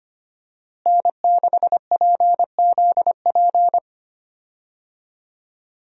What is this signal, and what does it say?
Morse code, heard as a keyed tone, about 25 wpm, 700 Hz. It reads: N6PZP
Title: USA Call Signs - 5 Characters - Repeated 2x - 25wpm